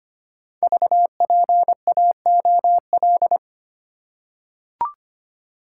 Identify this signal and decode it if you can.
Morse code, heard as a keyed tone, about 25 wpm, 700 Hz. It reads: VPAOL
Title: Code Groups - 5 Letters - 25wpm